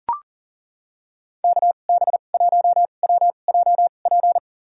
Morse code, heard as a keyed tone, about 40 wpm, spaced as for 29 wpm, 700 Hz. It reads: KB1WJP
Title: USA Call Signs - 2x Character Spacing - 40wpm